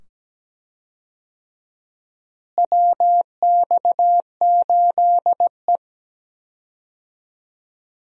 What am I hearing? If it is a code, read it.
Morse code, heard as a keyed tone, about 17 wpm, 700 Hz. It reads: WX8E
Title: USA Call Signs - 4 Characters - Repeated 2x - 17wpm